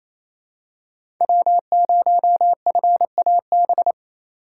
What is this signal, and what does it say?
Morse code, heard as a keyed tone, about 28 wpm, 700 Hz. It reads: W0FAB
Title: USA Call Signs - 5 Characters - 8x Character Spacing - 28wpm